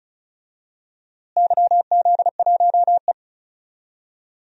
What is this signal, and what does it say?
Morse code, heard as a keyed tone, about 36 wpm, 700 Hz. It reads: YZ1E
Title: DXCC-100 Call Signs - 4 Characters - Repeated 2x - 35wpm